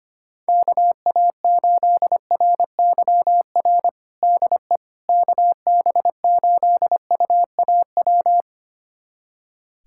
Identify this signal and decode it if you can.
Morse code, heard as a keyed tone, about 25 wpm, 700 Hz. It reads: KA8RYR DE KB8UAW
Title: USA Callsign DE Callsign 25wpm